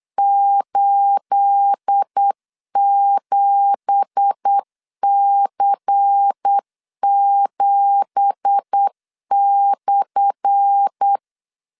Morse code, heard as a keyed tone, about 8 wpm, 800 Hz. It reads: 87C7/
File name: LEZIONI / 10 SIGNS / 10A SLASH 8wpm 3452